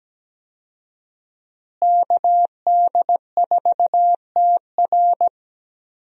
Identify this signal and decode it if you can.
Morse code, heard as a keyed tone, about 17 wpm, 700 Hz. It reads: KD4TR
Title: USA Call Signs - 5 Characters - Repeated 3x - 17wpm